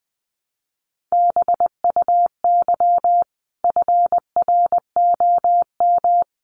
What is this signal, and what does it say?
Morse code, heard as a keyed tone, about 20 wpm, 700 Hz. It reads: BUY FROM
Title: Sentences from Top 700 Words 20wpm